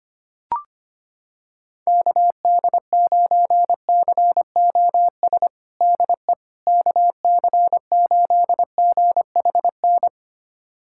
Morse code, heard as a keyed tone, about 25 wpm, 700 Hz. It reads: KD9COS DE KC8GHN
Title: USA Callsign DE Callsign - 6 Characters - 25wpm